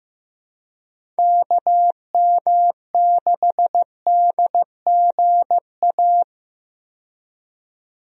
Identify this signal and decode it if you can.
Morse code, heard as a keyed tone, about 15 wpm, 700 Hz. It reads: KM6DGA
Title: USA Call Signs - 6 Characters - 15wpm